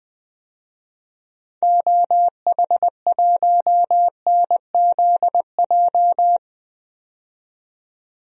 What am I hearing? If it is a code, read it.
Morse code, heard as a keyed tone, about 20 wpm, 700 Hz. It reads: OH1NZJ
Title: DXCC-100 Call Signs - 6 Characters - 20wpm